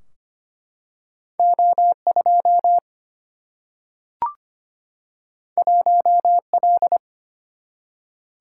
Morse code, heard as a keyed tone, about 25 wpm, 700 Hz. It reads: O2 1L
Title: Single-Letter Single-Number 25wpm